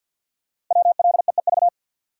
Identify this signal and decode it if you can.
Morse code, heard as a keyed tone, about 52 wpm, 700 Hz. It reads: WLEEV
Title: Code Groups - 5 Letters - 50wpm